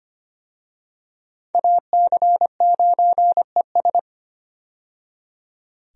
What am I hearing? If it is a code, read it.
Morse code, heard as a keyed tone, about 25 wpm, 700 Hz. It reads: AC9ES
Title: USA Call Signs - 5 Characters - 25wpm